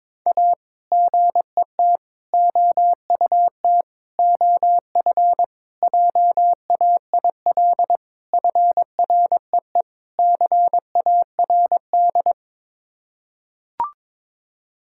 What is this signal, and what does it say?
Morse code, heard as a keyed tone, about 22 wpm, 700 Hz. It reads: A GET OUT OF JAIL FREE CARD
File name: practice / 22wpm / Sets of 7 Words - 22wpm